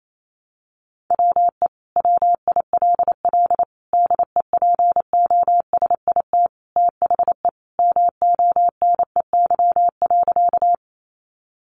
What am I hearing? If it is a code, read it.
Morse code, heard as a keyed tone, about 28 wpm, 700 Hz. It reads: WE WILL DEPOSIT THE MONEY.
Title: Sentences from Top 1300 Words 28wpm